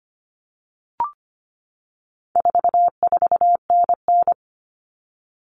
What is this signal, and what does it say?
Morse code, heard as a keyed tone, about 25 wpm, 700 Hz. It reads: 44NN
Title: Common Cut Numbers - 4-Digit 25wpm